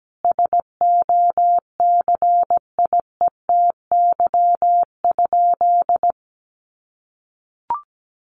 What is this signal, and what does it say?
Morse code, heard as a keyed tone, about 17 wpm, 700 Hz. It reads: SOCIETY?
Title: Sentences from Top 700 Words 17wpm